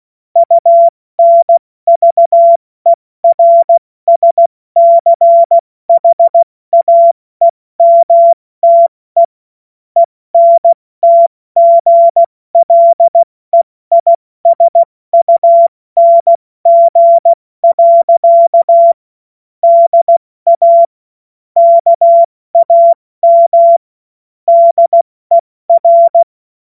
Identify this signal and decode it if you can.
Morse code, heard as a keyed tone, about 16 wpm, 680 Hz. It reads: UNVERSCHAEMTE ENTGLEISUNG. DA KAM DER
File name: Morsekurs / Witze / Witz 05 Dezember 2023 70 F80